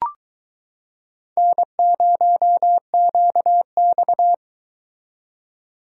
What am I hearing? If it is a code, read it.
Morse code, heard as a keyed tone, about 22 wpm, 700 Hz. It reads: N0QX